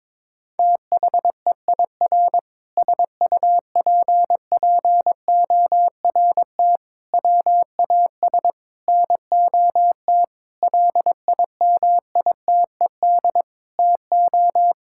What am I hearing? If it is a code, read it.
Morse code, heard as a keyed tone, about 22 wpm, 700 Hz. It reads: THEIR SUPPORT WAS NOT LIMITED TO
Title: Sentences from Top 1000 Words - Review Part 6 - 22wpm